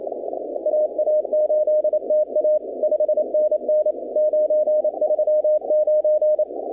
Morse code, heard as a keyed tone, about 28 wpm, 610 Hz. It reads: AA8TA 5NN 939
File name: contests / 2019-cw-wpx-kt5j-k5pi- -2330 0-20